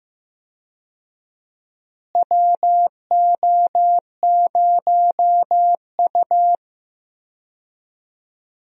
Morse code, heard as a keyed tone, about 15 wpm, 700 Hz. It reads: WO0U
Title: USA Call Signs - 4 Characters - Repeated 3x - 15wpm